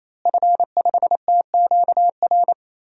Morse code, heard as a keyed tone, about 28 wpm, 700 Hz. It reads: F5TQR
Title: DXCC-100 Call Signs - 5 Characters - Repeated 3x - 28wpm